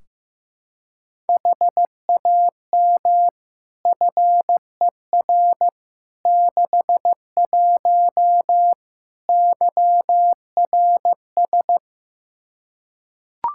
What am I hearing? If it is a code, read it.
Morse code, heard as a keyed tone, about 15 wpm, 700 Hz. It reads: HAM FER 61 YRS
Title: K1USN SST QSO Practice 15wpm